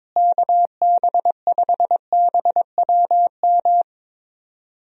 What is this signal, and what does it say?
Morse code, heard as a keyed tone, about 22 wpm, 700 Hz. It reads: KB5BWM
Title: USA Call Signs - 6 Characters - Repeated 2x - 22wpm